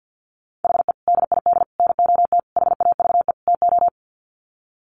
Morse code, heard as a keyed tone, about 50 wpm, 700 Hz. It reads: HE DID NOT HAVE TO
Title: Sets of 5 Words 50wpm